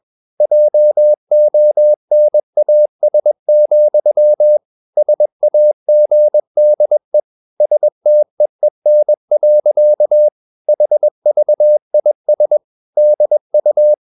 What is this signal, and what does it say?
Morse code, heard as a keyed tone, about 21 wpm, 600 Hz. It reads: JONAS, SAGDE STEEN. HVIS DU
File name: Ferie QRS 21wpm